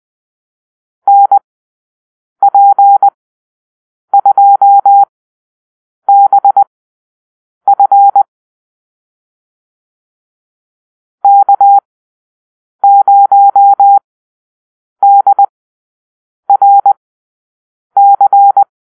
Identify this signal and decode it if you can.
Morse code, heard as a keyed tone, about 20 wpm, spaced as for 7 wpm, 800 Hz. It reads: NP2BF K0DRC
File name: arrl / Call-Sign Drill-8